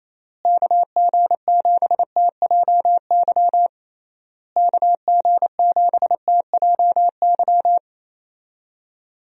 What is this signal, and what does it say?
Morse code, heard as a keyed tone, about 28 wpm, 700 Hz. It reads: KG7TJY KG7TJY
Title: USA Call Signs - 6 Characters - Repeated 3x - 28wpm